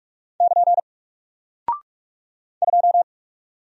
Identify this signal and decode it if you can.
Morse code, heard as a keyed tone, about 45 wpm, 700 Hz. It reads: <KN> 2
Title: Single Character (Letter:Number:Punctuation:Prosign) 45wpm